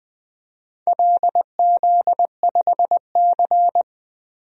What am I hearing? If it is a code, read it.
Morse code, heard as a keyed tone, about 20 wpm, 700 Hz. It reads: LZ5C
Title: DXCC-100 Call Signs - 4 Characters - Repeated 3x - 20wpm